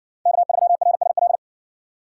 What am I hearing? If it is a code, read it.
Morse code, heard as a keyed tone, about 57 wpm, 700 Hz. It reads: K3RSL